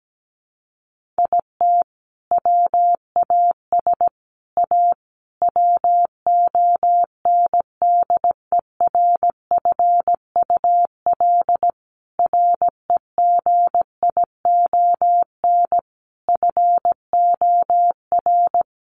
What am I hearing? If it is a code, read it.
Morse code, heard as a keyed tone, about 17 wpm, 700 Hz. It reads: IT WAS A WONDERFUL REGION FOR
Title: Sentences from Top 1500 Words 17wpm